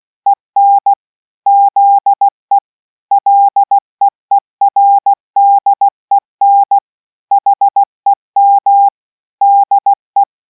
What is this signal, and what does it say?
Morse code, heard as a keyed tone, about 16 wpm, 800 Hz. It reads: EN ZE LEERDEN HEM DE